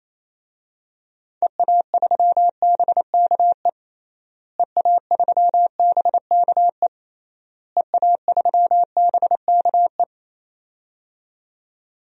Morse code, heard as a keyed tone, about 27 wpm, 700 Hz. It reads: EA3BKE EA3BKE EA3BKE
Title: DXCC-100 Call Signs - 6 Characters - Repeated 3x - 28wpm